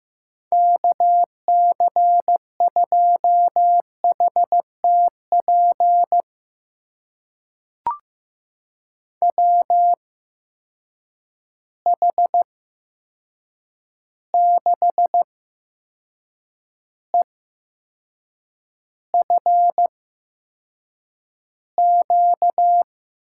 Morse code, heard as a keyed tone, about 15 wpm, spaced as for 6 wpm, 700 Hz. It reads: KC2HTP WH6EFQ
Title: USA Call Signs - 6 Characters - 8x Character Spacing - 15wpm